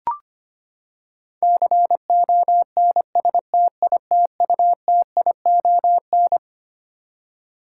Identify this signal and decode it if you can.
Morse code, heard as a keyed tone, about 25 wpm, 700 Hz. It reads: CONSTITUTION
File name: Twelve Letter Words - 25wpm